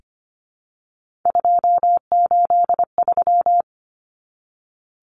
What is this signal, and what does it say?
Morse code, heard as a keyed tone, about 25 wpm, 700 Hz. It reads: 283
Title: Numbers - 3-Digit 25wpm